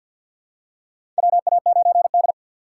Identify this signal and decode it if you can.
Morse code, heard as a keyed tone, about 53 wpm, 690 Hz. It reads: WA9D
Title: USA Call Signs - 6x Character Spacing - 50wpm